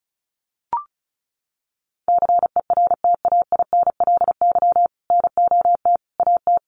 Morse code, heard as a keyed tone, about 34 wpm, 700 Hz. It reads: CERTAINLY NOT AT
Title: Sentences from Top 900 Words 35wpm